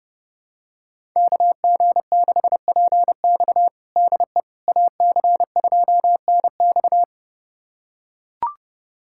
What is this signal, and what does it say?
Morse code, heard as a keyed tone, about 32 wpm, 700 Hz. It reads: KG6PX DE AC2NX
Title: USA Callsign DE Callsign - 5 Characters - 30wpm